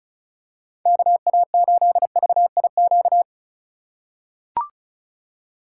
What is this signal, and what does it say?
Morse code, heard as a keyed tone, about 36 wpm, 700 Hz. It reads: KA8VIQ
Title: USA Call Signs - 6 Characters - 2x Character Spacing - 35wpm